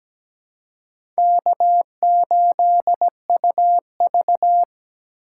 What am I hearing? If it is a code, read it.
Morse code, heard as a keyed tone, about 17 wpm, 700 Hz. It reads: K8UV
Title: USA Call Signs - 4 Characters - 3x Character Spacing - 17wpm